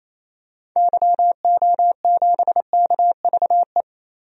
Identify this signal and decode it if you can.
Morse code, heard as a keyed tone, about 28 wpm, 700 Hz. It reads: YO7KVE
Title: DXCC-100 Call Signs - 6 Characters - Repeated 3x - 28wpm